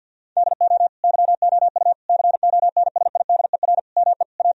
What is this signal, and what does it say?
Morse code, heard as a keyed tone, about 50 wpm, 700 Hz. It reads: DO YOU CONSIDER ME A
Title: Sentences from Top 1000 Words - Review Part 4 - 50wpm